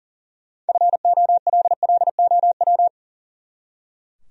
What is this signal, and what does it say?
Morse code, heard as a keyed tone, about 50 wpm, 700 Hz. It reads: FOLLOW